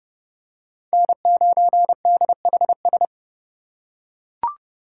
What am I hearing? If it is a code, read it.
Morse code, heard as a keyed tone, about 29 wpm, 700 Hz. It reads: N9DHS